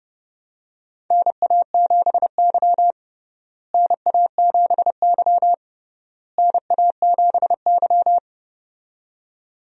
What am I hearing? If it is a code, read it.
Morse code, heard as a keyed tone, about 32 wpm, 700 Hz. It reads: NA7Y NA7Y NA7Y
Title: USA Call Signs - 4 Characters - Repeated 3x - 30wpm